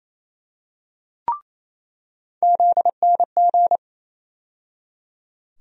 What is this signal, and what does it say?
Morse code, heard as a keyed tone, about 28 wpm, 700 Hz. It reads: ZNG
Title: USA Call Sign Postfixes - 3 Letters 28wpm